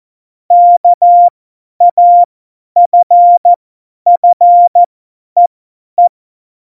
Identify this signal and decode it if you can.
Morse code, heard as a keyed tone, about 14 wpm, spaced as for 10 wpm, 700 Hz. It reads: KAFFEE
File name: Morsekurs / Witze / Witz 10. September 2019, 50 F70